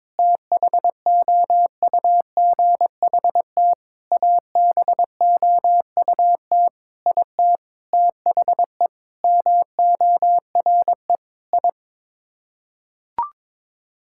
Morse code, heard as a keyed tone, about 21 wpm, 700 Hz. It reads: THOUGHT ABOUT IT THE MORE I
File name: Sets of 9 Words - 22wpm